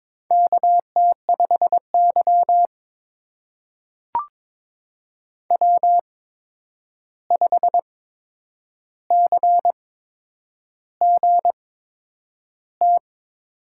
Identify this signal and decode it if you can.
Morse code, heard as a keyed tone, about 22 wpm, spaced as for 9 wpm, 700 Hz. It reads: KT5Y W5CGT
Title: USA Call Signs - 8x Character Spacing - 22wpm